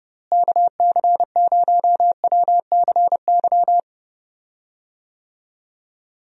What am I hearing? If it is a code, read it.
Morse code, heard as a keyed tone, about 30 wpm, 700 Hz. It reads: KC0WCY
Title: USA Call Signs - 6 Characters - Repeated 3x - 30wpm